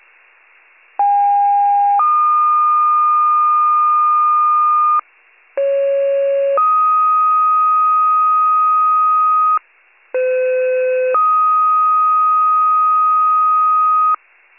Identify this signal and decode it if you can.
Two-tone page, heard as a one-second tone, then a three-second tone, about 810 then 1220 Hz, about 550 then 1250 Hz, about 510 then 1250 Hz. It is heard from 1.0 to 14.2 s.